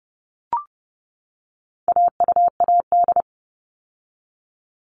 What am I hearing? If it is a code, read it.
Morse code, heard as a keyed tone, about 30 wpm, 700 Hz. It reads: AUAD
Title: All Cut Numbers - 4-Digit 30wpm